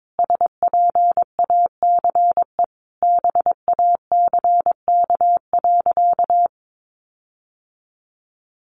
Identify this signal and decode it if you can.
Morse code, heard as a keyed tone, about 22 wpm, 700 Hz. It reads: SPACE BACK.
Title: Sentences from Top 1500 Words 22wpm